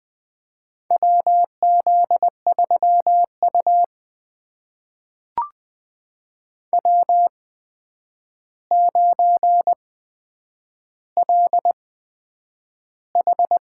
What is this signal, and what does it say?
Morse code, heard as a keyed tone, about 20 wpm, spaced as for 9 wpm, 700 Hz. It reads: WZ3U W9LH
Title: USA Call Signs - 4 Characters - 8x Character Spacing - 20wpm